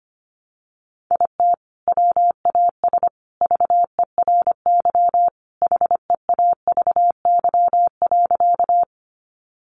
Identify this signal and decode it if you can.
Morse code, heard as a keyed tone, about 25 wpm, 700 Hz. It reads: IT WAS VERY HEAVY.
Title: Sentences from Top 1100 Words 25wpm